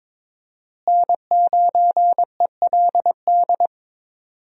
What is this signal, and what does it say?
Morse code, heard as a keyed tone, about 22 wpm, 700 Hz. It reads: N9ELD
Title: USA Call Signs - 5 Characters - Repeated 2x - 22wpm